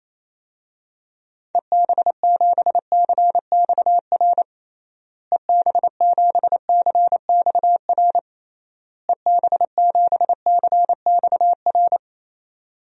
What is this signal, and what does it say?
Morse code, heard as a keyed tone, about 28 wpm, 700 Hz. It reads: EB7CXR EB7CXR EB7CXR
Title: DXCC-100 Call Signs - 6 Characters - Repeated 3x - 28wpm